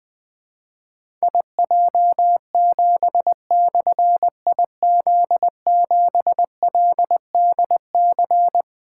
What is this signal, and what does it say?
Morse code, heard as a keyed tone, about 20 wpm, 700 Hz. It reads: IJ7/IZ7LDC
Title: Difficult Call Signs 20wpm